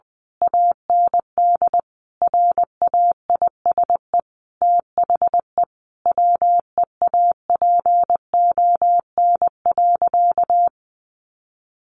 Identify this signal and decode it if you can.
Morse code, heard as a keyed tone, about 19 wpm, 700 Hz. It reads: AND RAISE THE WEAPON.
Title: A Sound Of Thunder 20wpm